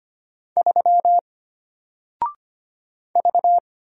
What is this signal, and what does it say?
Morse code, heard as a keyed tone, about 25 wpm, 700 Hz. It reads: 3 V
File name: Single-Letter-Number-25wpm